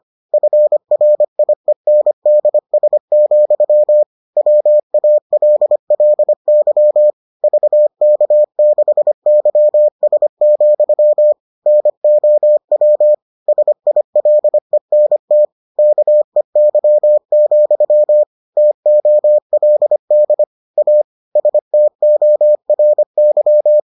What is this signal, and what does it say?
Morse code, heard as a keyed tone, about 25 wpm, 600 Hz. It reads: FRIENDS, WALLY VK6YS, NOW SILENT KEY, TOLD A STORY